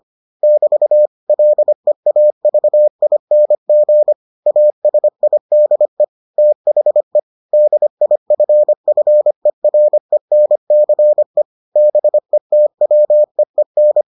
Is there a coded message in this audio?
Yes: Morse code from 0.4 to 14.0 s, heard as a keyed tone, about 25 wpm, 600 Hz. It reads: <BT> LEAVING ASIDE THE DIFFERENCE BETWEEN